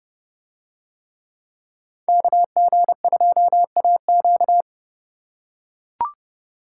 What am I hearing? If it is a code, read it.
Morse code, heard as a keyed tone, about 30 wpm, 700 Hz. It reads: KG2AQ